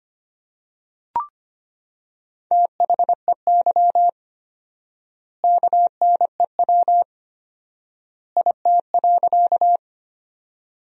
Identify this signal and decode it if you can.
Morse code, heard as a keyed tone, about 25 wpm, 700 Hz. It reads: THEY KNEW IT.
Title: 4x Word-Spacing - Sentences from Top 400 Words - 25wpm